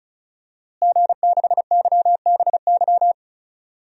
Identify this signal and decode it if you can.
Morse code, heard as a keyed tone, about 36 wpm, 700 Hz. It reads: G6YBY
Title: DXCC-100 Call Signs - 5 Characters - Repeated 2x - 35wpm